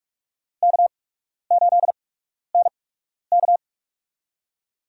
Morse code, heard as a keyed tone, about 41 wpm, spaced as for 12 wpm, 700 Hz. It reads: K8NK
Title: USA Call Signs - 4 Characters - 8x Character Spacing - 45wpm